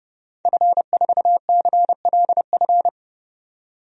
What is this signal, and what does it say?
Morse code, heard as a keyed tone, about 30 wpm, 690 Hz. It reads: F4CLF
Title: DXCC-100 Call Signs - 5 Characters - 30wpm